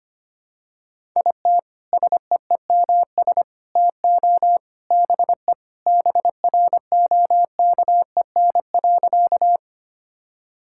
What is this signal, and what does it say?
Morse code, heard as a keyed tone, about 25 wpm, 700 Hz. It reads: IT SEEMS TO BE BROKEN.
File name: practice / 25wpm / Sentences with 3-Word Phrases - 25wpm